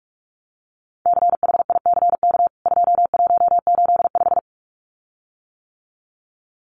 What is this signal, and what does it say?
Morse code, heard as a keyed tone, about 44 wpm, 700 Hz. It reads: CHICK 2185
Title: CWT Exchange 45wpm